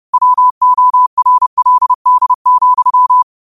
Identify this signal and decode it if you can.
Morse code, heard as a keyed tone, about 30 wpm, 1000 Hz. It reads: WORLD,
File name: qotdfiles / 30wpm